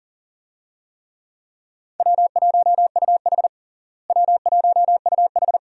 Morse code, heard as a keyed tone, about 40 wpm, 700 Hz. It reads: W1UH W1UH
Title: USA Call Signs - 4 Characters - Repeated 2x - 40wpm